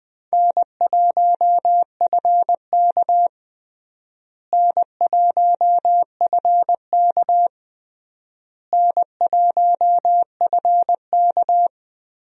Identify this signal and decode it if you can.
Morse code, heard as a keyed tone, about 20 wpm, 700 Hz. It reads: N1FK N1FK N1FK
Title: USA Call Signs - 4 Characters - Repeated 3x - 20wpm